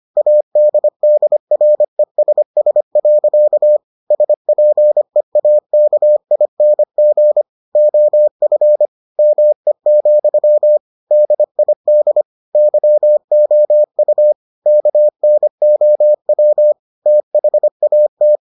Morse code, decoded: ADDRESS. SPEAKING OF ME, DID YOU KNOW THAT

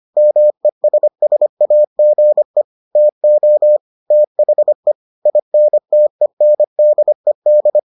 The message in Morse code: MESSAGE TO THE INTENDED

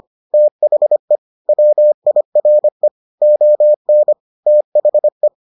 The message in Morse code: THE WIRE ON THE

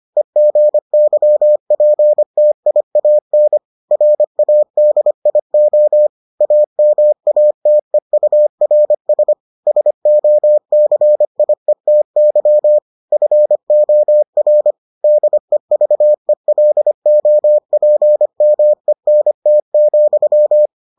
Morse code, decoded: EGYPTIAN RADIO AMATEURS SOCIETY FOR DEVELOPMENT,

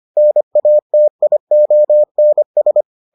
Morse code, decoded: NATIONS

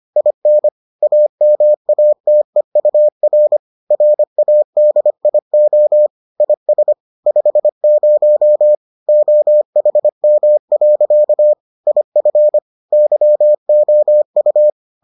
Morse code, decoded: IN AMATEUR RADIO IS 50 OHM. IF YOU

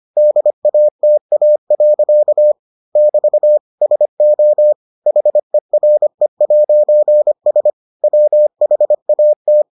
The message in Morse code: DATA. <BT> SO HERE'S WHAT

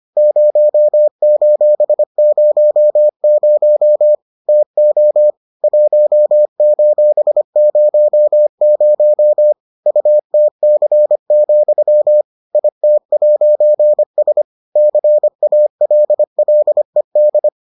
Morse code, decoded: 0:00 TO 1:00 UTC, IT'S CALLED

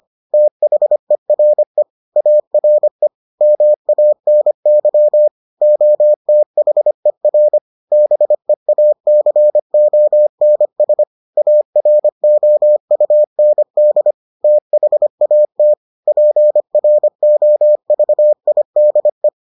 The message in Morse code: THERE ARE MANY OTHER BEACONS AROUND THAT PROVIDE

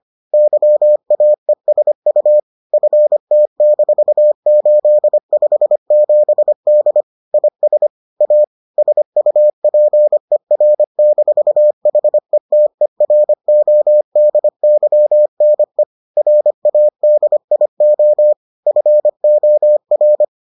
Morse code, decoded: YAESU FT-857D IS A SUPER-HETERODYNE RADIO FOR

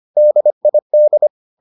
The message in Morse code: DID

